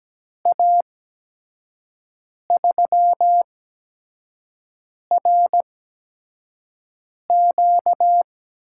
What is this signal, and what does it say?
Morse code, heard as a keyed tone, about 17 wpm, spaced as for 5 wpm, 700 Hz. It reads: A3RQ
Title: USA Call Signs - 5 Characters - 8x Character Spacing - 17wpm